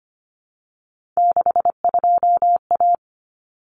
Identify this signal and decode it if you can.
Morse code, heard as a keyed tone, about 25 wpm, 700 Hz. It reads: 62A
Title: Common Cut Numbers - 3-Digit 25wpm